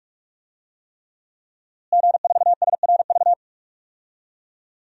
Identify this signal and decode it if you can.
Morse code, heard as a keyed tone, about 41 wpm, 700 Hz. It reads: G4SRV
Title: DXCC-100 Call Signs - 5 Characters - Repeated 3x - 45wpm